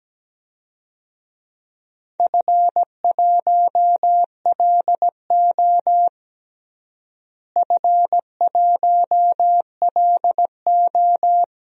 Morse code, heard as a keyed tone, about 17 wpm, 700 Hz. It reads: F1LO F1LO
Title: DXCC-100 Call Signs - 4 Characters - Repeated 2x - 17wpm